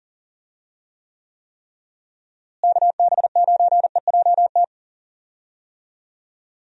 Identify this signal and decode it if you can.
Morse code, heard as a keyed tone, about 40 wpm, 700 Hz. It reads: KB9EJT
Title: USA Call Signs - 6 Characters - Repeated 3x - 40wpm